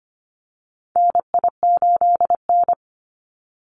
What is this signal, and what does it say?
Morse code, heard as a keyed tone, about 25 wpm, 700 Hz. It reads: NI8N
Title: Call Signs Repeated 2x - 25wpm